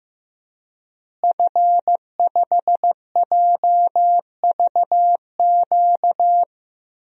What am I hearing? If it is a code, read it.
Morse code, heard as a keyed tone, about 15 wpm, 700 Hz. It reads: F5JVQ
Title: DXCC-100 Call Signs - 5 Characters - Repeated 2x - 15wpm